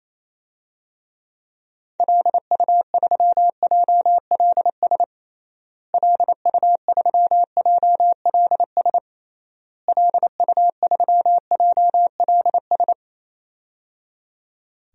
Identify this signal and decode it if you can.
Morse code, heard as a keyed tone, about 28 wpm, 700 Hz. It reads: LU3JLS LU3JLS LU3JLS
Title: DXCC-100 Call Signs - 6 Characters - Repeated 3x - 28wpm